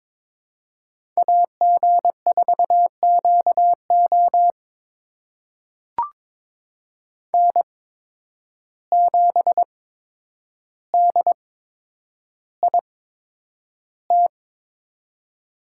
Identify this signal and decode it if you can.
Morse code, heard as a keyed tone, about 22 wpm, spaced as for 9 wpm, 700 Hz. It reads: AG4QO N7DIT
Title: USA Call Signs - 5 Characters - 8x Character Spacing - 22wpm